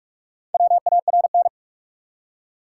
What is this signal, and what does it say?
Morse code, heard as a keyed tone, about 46 wpm, 700 Hz. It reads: WARN
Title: Four Letter Words - 45wpm